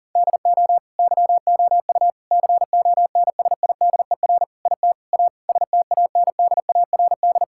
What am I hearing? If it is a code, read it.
Morse code, heard as a keyed tone, about 37 wpm, 700 Hz. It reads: DO YOU CONSIDER IT A STANDARD
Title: Sentences from Top 1000 Words - Review Part 6 - 40wpm